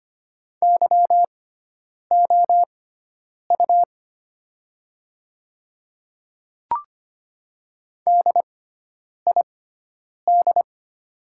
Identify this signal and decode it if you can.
Morse code, heard as a keyed tone, about 25 wpm, spaced as for 9 wpm, 690 Hz. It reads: YOU DID
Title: 6x Character Spacing - Top 200 Words - 25wpm